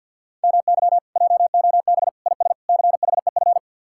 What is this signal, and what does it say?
Morse code, heard as a keyed tone, about 66 wpm, spaced as for 58 wpm, 700 Hz. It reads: MY JOB IS CHEF
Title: Speed-Racing Common QSO Phrases Encore - 50wpm